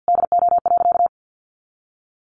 Morse code, heard as a keyed tone, about 50 wpm, 700 Hz. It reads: DO.
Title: Additional Easy Phrases 50wpm